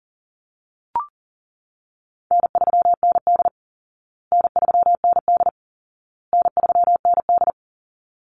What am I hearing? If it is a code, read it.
Morse code, heard as a keyed tone, about 40 wpm, 700 Hz. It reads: N3ND N3ND N3ND
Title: Easy Call Signs Repeated 3x - 40wpm